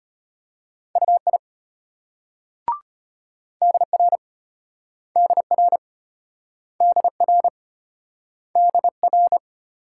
Morse code, heard as a keyed tone, about 34 wpm, 700 Hz. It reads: UI DR DR DR DR